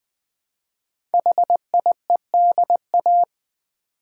Morse code, heard as a keyed tone, about 20 wpm, 700 Hz. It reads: HIEDA
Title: Code Groups - 5 Letters - 20wpm